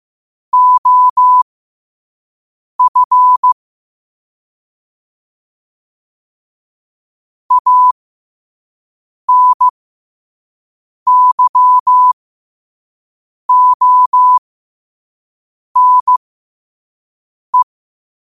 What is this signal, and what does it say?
Morse code, heard as a keyed tone, about 15 wpm, spaced as for 5 wpm, 1000 Hz. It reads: OF ANYONE